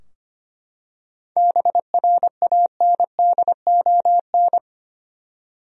Morse code, heard as a keyed tone, about 25 wpm, 700 Hz. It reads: BRANDON
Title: USA Common Names - 25wpm